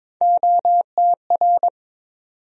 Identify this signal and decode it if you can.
Morse code, heard as a keyed tone, about 22 wpm, 700 Hz. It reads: OTR
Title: USA Call Sign Postfixes - 3 Letters 22wpm